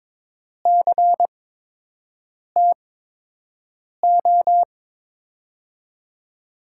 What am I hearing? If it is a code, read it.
Morse code, heard as a keyed tone, about 22 wpm, spaced as for 6 wpm, 700 Hz. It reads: CTO
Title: USA Call Signs - 6 Characters - 8x Character Spacing - 22wpm